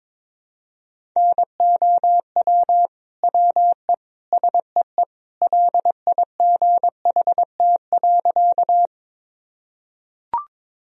Morse code, heard as a keyed tone, about 22 wpm, 700 Hz. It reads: NOW WE SEE LIGHT.
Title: Sentences from Top 1000 Words - Review Part 1 - 22wpm